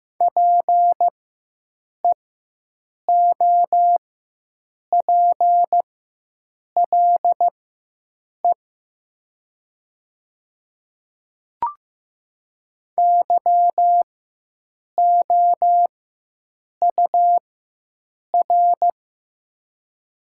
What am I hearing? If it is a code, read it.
Morse code, heard as a keyed tone, about 15 wpm, spaced as for 7 wpm, 700 Hz. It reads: PEOPLE YOUR
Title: 4x Character Spacing - Top 100 Words - 15wpm